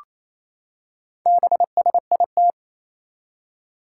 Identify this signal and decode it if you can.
Morse code, heard as a keyed tone, about 27 wpm, 700 Hz. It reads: BSIT
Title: Four Challenging Letter-Number Combinations 28wpm